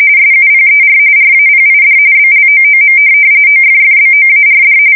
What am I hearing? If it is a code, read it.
RTTY, about 45.45 baud. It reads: SUPER RAPID BLOOM OFFBOARD
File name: fsk170 / USS-New-Jersey